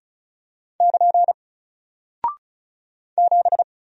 Morse code, heard as a keyed tone, about 36 wpm, 700 Hz. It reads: <KN> 7
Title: Single Character (Letter:Number:Punctuation:Prosign) Alt 35wpm